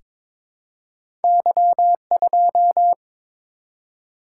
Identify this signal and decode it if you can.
Morse code, heard as a keyed tone, about 22 wpm, 700 Hz. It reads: Y2
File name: practice / 22wpm / Two Letter-Number Combinations - 22wpm